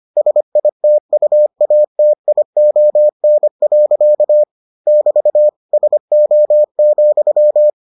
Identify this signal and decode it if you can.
Morse code, heard as a keyed tone, about 25 wpm, 600 Hz. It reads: SITUATION. <BT> SO,